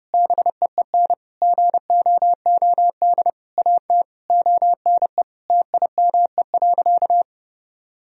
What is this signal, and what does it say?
Morse code, heard as a keyed tone, about 30 wpm, 700 Hz. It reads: BEEN GOOD AT ONE TIME.
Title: Sentences from Top 1000 Words - Review Part 2 - 30wpm